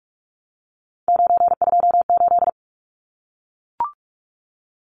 Morse code, heard as a keyed tone, about 44 wpm, 700 Hz. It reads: 928
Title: Numbers - 3-Digit 45wpm